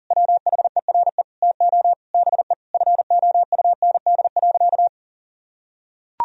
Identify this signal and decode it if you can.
Morse code, heard as a keyed tone, about 37 wpm, 700 Hz. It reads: WHERE TO BE FOUND.
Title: Sentences from Top 1000 Words - Review Part 3 - 40wpm